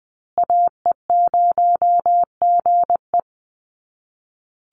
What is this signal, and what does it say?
Morse code, heard as a keyed tone, about 20 wpm, 700 Hz. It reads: AE0GE
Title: USA Call Signs Repeated 2x 20wpm